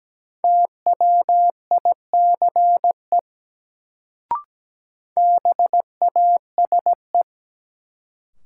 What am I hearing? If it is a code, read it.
Morse code, heard as a keyed tone, about 17 wpm, 700 Hz. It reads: TWICE BASE
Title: Sentences from Top 1600 Words - New Word Drill Alt - 17wpm